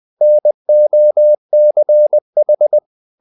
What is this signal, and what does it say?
Morse code, heard as a keyed tone, about 20 wpm, 600 Hz. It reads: NOCH
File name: dk5ke / Humor-100BpM